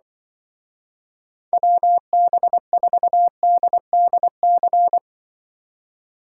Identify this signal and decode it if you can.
Morse code, heard as a keyed tone, about 25 wpm, 700 Hz. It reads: WB4DDC
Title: Speed Racing - USA 2x3 Call Signs 20wpm